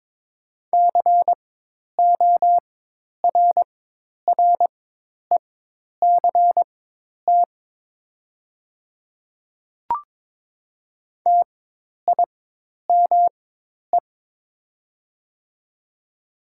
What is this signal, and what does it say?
Morse code, heard as a keyed tone, about 22 wpm, spaced as for 10 wpm, 700 Hz. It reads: CORRECT TIME
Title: 4x Character Spacing - Top 500 Words - 22wpm